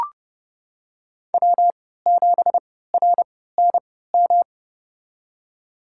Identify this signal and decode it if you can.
Morse code, heard as a keyed tone, about 30 wpm, spaced as for 17 wpm, 700 Hz. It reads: W7RNM
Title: USA Call Signs - 5 Characters - 3x Character Spacing - 30wpm